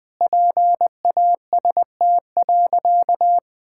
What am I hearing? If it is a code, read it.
Morse code, heard as a keyed tone, about 19 wpm, 700 Hz. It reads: PAST.
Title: Sentences with 2-Word Phrases - 20wpm